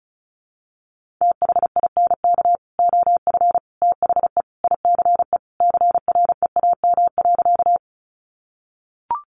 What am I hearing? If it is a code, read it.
Morse code, heard as a keyed tone, about 34 wpm, 700 Hz. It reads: THINK OF THE ICE CREAM.
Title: Sentences from Top 1500 Words 35wpm